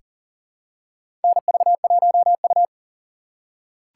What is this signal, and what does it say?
Morse code, heard as a keyed tone, about 40 wpm, 700 Hz. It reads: NV1U
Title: K1USN SST QSO Practice - 40wpm